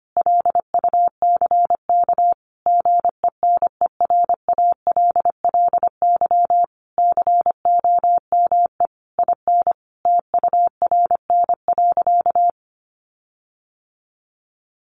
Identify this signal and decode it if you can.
Morse code, heard as a keyed tone, about 25 wpm, 700 Hz. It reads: LUCK GENERALLY COME IN TURN.
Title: Sentences from Top 1500 Words 25wpm